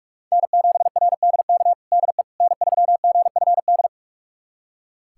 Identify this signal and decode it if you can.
Morse code, heard as a keyed tone, about 41 wpm, 700 Hz. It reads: N7RDK DE N3GFD
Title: USA Callsign DE Callsign - 5 Characters - 45wpm